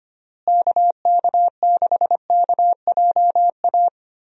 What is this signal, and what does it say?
Morse code, heard as a keyed tone, about 25 wpm, 700 Hz. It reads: KK6KJA
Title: USA Call Signs - 6 Characters - Repeated 3x - 25wpm